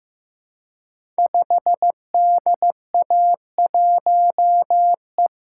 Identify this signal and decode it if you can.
Morse code, heard as a keyed tone, about 15 wpm, 700 Hz. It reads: 5DA1E
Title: Code Groups - 5 Letter:Numbers - 15wpm